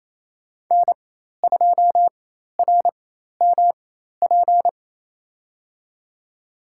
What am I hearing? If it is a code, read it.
Morse code, heard as a keyed tone, about 28 wpm, spaced as for 13 wpm, 700 Hz. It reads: N2RMP
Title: USA Call Signs - 5 Characters - 4x Character Spacing - 28wpm